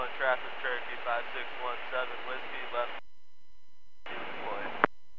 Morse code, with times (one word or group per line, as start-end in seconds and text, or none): none